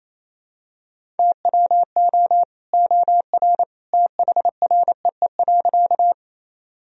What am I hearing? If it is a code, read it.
Morse code, heard as a keyed tone, about 28 wpm, 700 Hz. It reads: TWO OR THREE.
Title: Sentences from Top 1000 Words - Review Part 1 - 28wpm